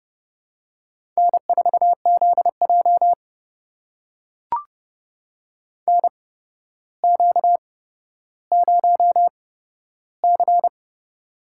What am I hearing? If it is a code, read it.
Morse code, heard as a keyed tone, about 30 wpm, spaced as for 13 wpm, 700 Hz. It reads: N4ZJ NQ0C
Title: USA Call Signs - 4 Characters - 8x Character Spacing - 30wpm